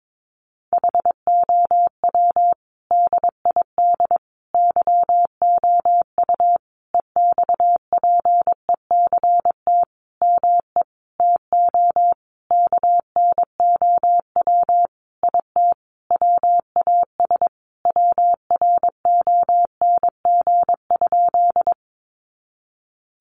Morse code, heard as a keyed tone, about 22 wpm, 700 Hz. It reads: HOW DID YOU EXPECT ME TO KNOW IT WAS WRONG?
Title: Sentences from Top 1100 Words 22wpm